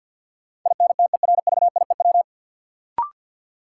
Morse code, heard as a keyed tone, about 47 wpm, 700 Hz. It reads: INTERVIEW
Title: Sentences from Top 1600 Words - New Word Drill Alt - 50wpm